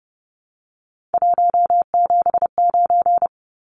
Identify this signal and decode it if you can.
Morse code, heard as a keyed tone, about 30 wpm, 700 Hz. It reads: 179
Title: Numbers - 3-Digit 30wpm